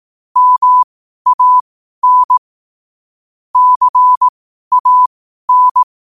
Morse code, heard as a keyed tone, about 18 wpm, spaced as for 12 wpm, 1000 Hz. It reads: MAN CAN
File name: qotdfiles / 12wpm